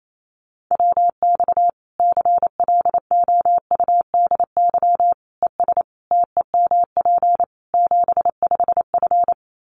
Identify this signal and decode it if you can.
Morse code, heard as a keyed tone, about 28 wpm, 700 Hz. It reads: WX CLOUDY ES TEMP 75F
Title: Common QSO Phrases 28wpm